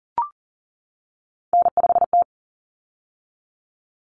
Morse code, heard as a keyed tone, about 40 wpm, 700 Hz. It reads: N5T
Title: Common Cut Numbers - 3-Digit 40wpm